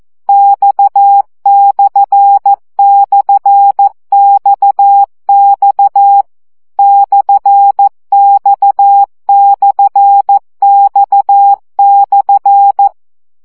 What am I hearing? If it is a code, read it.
Morse code, heard as a keyed tone, about 14 wpm, 800 Hz. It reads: X//XX /X/X/
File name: CW gyakorlas x per 12 baud